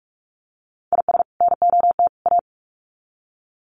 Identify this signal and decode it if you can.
Morse code, heard as a keyed tone, about 44 wpm, 700 Hz. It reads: IS NOT A
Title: Sets of 3 Words 45wpm